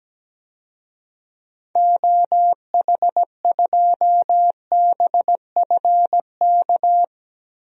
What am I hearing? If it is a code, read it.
Morse code, heard as a keyed tone, about 17 wpm, 700 Hz. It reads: OH2BFK
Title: DXCC-100 Call Signs - 6 Characters - 17wpm